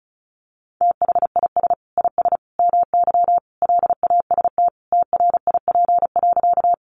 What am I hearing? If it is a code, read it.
Morse code, heard as a keyed tone, about 34 wpm, 700 Hz. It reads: THIS IS MY LAST TRIP.
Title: Sentences from Top 1300 Words 35wpm